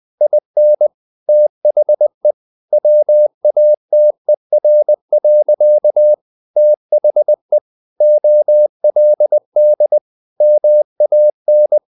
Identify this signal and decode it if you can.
Morse code, heard as a keyed tone, about 20 wpm, 600 Hz. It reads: IN THE WATER. THE OLD MAN
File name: the old man and the sea part 2 20wpm